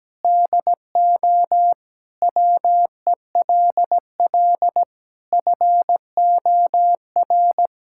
Morse code, heard as a keyed tone, about 17 wpm, 700 Hz. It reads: DO WELL FOR
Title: Sentences from Top 1000 Words - Review Part 5 - 17wpm